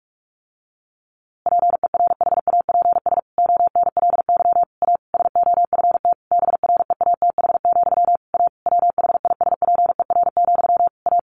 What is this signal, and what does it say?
Morse code, heard as a keyed tone, about 44 wpm, 700 Hz. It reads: PERHAPS ONLY A SOFT BREATH, A WHISPER, A
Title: A Sound Of Thunder 45wpm